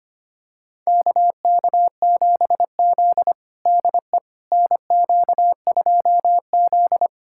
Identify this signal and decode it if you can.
Morse code, heard as a keyed tone, about 25 wpm, 700 Hz. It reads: KK7Z DE NQ2Z
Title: USA Callsign DE Callsign - 4 Characters - 25wpm